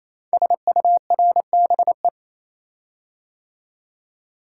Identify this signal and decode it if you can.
Morse code, heard as a keyed tone, about 28 wpm, 700 Hz. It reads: SURBE
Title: Code Groups - 5 Letters - 28wpm